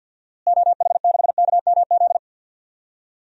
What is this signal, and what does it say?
Morse code, heard as a keyed tone, about 46 wpm, 700 Hz. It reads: OH6KMZ
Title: DXCC-100 Call Signs - 6 Characters - Repeated 3x - 50wpm